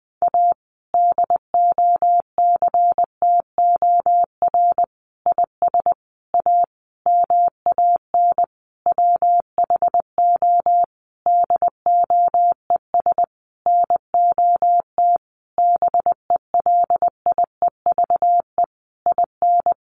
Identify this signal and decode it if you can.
Morse code, heard as a keyed tone, about 20 wpm, 700 Hz. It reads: A DOCTOR IS A MAN WHO DOES NOT BELIEVE IN
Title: Sentences from Top 1500 Words 20wpm